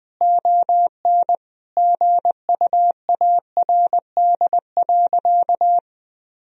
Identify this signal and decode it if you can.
Morse code, heard as a keyed tone, about 21 wpm, 700 Hz. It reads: ON GUARD.
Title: Sentences from Top 1000 Words - Review Part 4 - 20wpm